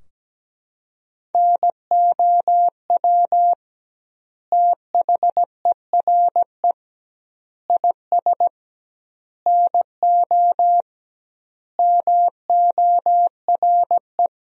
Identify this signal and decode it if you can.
Morse code, heard as a keyed tone, about 17 wpm, 700 Hz. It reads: NOW THERE IS NO MORE
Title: 2x Word-Spacing - Sentences from Top 100 Words - 17wpm